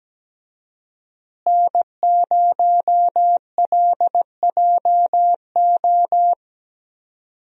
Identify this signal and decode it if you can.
Morse code, heard as a keyed tone, about 17 wpm, 700 Hz. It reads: N0LJO
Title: USA Call Signs - 5 Characters - Repeated 3x - 17wpm